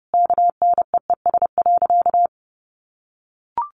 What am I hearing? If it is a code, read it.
Morse code, heard as a keyed tone, about 30 wpm, 700 Hz. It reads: KNEES.
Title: A Sound Of Thunder 30wpm